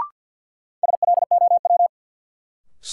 Morse code, heard as a keyed tone, about 52 wpm, 700 Hz. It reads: SLOW
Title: Sentences from Top 400 Words - New Word Drill - 50wpm